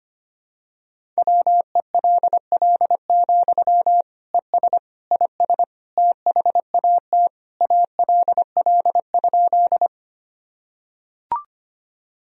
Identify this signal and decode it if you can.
Morse code, heard as a keyed tone, about 25 wpm, 700 Hz. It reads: WELL, ES IS THAT ALL?
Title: Sentences from Top 200 Words and Abbreviations 25wpm